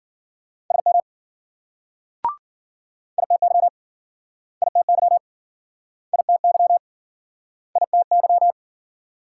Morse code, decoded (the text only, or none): SM ITY ITY ITY ITY